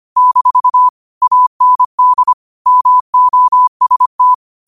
Morse code, heard as a keyed tone, about 25 wpm, 1000 Hz. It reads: - AND MOST
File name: qotdfiles / 25wpm